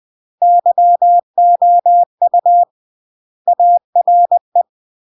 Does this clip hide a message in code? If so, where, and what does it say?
Yes: Morse code from 0.4 to 4.6 s, heard as a keyed tone, about 20 wpm, 700 Hz. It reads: YOU ARE